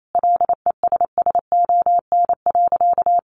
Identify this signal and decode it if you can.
Morse code, heard as a keyed tone, about 28 wpm, 700 Hz. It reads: LESSON.
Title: Sentences from Top 1200 Words 28wpm